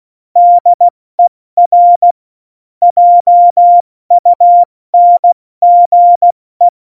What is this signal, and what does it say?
Morse code, heard as a keyed tone, about 16 wpm, 700 Hz. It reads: DER JUNGE
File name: Morsekurs / Witze / Witz 29. Januar 2019, 70 F80